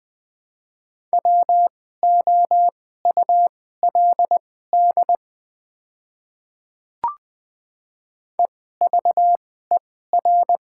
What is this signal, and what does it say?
Morse code, heard as a keyed tone, about 21 wpm, spaced as for 15 wpm, 690 Hz. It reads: WOULD EVER
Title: 2x Character Spacing - Top 500 Words - 20wpm